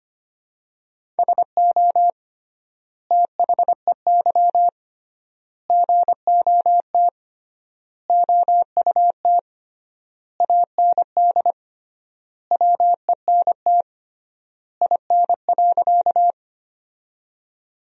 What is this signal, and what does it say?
Morse code, heard as a keyed tone, about 25 wpm, 700 Hz. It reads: SO THEY GOT OUT AND WENT IN.
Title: 3x Word-Spacing - Sentences from Top 300 Words - 25wpm